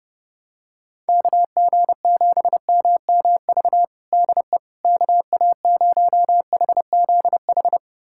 Morse code, heard as a keyed tone, about 30 wpm, 700 Hz. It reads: KG7MMV DE KA0HZH
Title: USA Callsign DE Callsign 30wpm